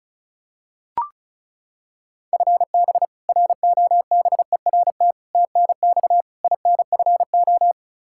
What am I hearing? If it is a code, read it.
Morse code, heard as a keyed tone, about 34 wpm, 700 Hz. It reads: FB ROBERT TNX INFO
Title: K1USN SST QSO Practice 35wpm